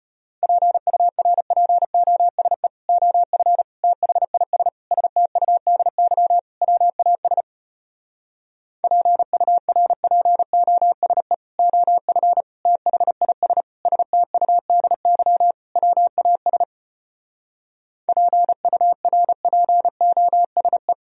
Morse code, heard as a keyed tone, about 33 wpm, 700 Hz. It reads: PURPOSE OF THIS STUDY WAS PURPOSE OF THIS STUDY WAS PURPOSE